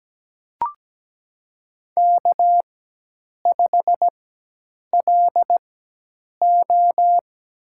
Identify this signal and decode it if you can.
Morse code, heard as a keyed tone, about 17 wpm, spaced as for 8 wpm, 700 Hz. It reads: K5LO
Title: USA Call Signs - 4 Characters - 4x Character Spacing - 17wpm